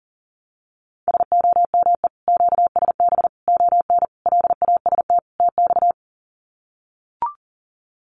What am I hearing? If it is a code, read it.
Morse code, heard as a keyed tone, about 40 wpm, 700 Hz. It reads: SOME QSB ON LAST TX
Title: Common QSO Phrases 40wpm